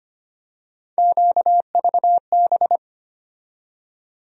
Morse code, heard as a keyed tone, about 25 wpm, 700 Hz. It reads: QVB
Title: USA Call Sign Postfixes - 3 Letters 25wpm